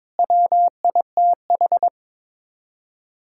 Morse code, heard as a keyed tone, about 22 wpm, 700 Hz. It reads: WITH
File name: practice / 22wpm / Sets of 7 Words - 22wpm